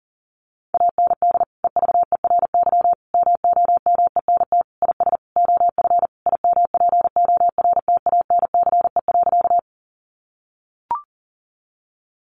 Morse code, heard as a keyed tone, about 40 wpm, 700 Hz. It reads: AND EVERY MOMENT IS OF IMPORTANCE.
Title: Sentences from Top 1200 Words 40wpm